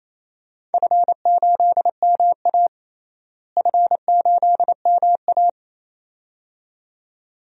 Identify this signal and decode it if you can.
Morse code, heard as a keyed tone, about 28 wpm, 700 Hz. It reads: F8MA F8MA
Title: DXCC-100 Call Signs - 4 Characters - Repeated 2x - 28wpm